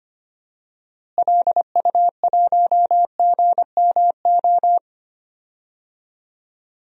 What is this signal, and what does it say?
Morse code, heard as a keyed tone, about 25 wpm, 700 Hz. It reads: LU1GMO
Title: DXCC-100 Call Signs - 6 Characters - Repeated 2x - 25wpm